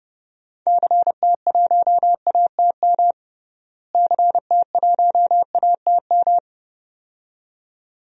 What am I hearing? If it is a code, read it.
Morse code, heard as a keyed tone, about 31 wpm, 700 Hz. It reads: CT1ATM CT1ATM
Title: DXCC-100 Call Signs - 6 Characters - Repeated 2x - 30wpm